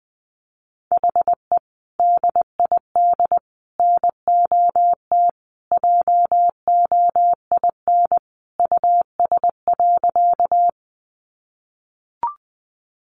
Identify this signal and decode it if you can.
Morse code, heard as a keyed tone, about 20 wpm, 700 Hz. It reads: HE DID NOT JOIN US.
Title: Sentences from Top 1400 Words 20wpm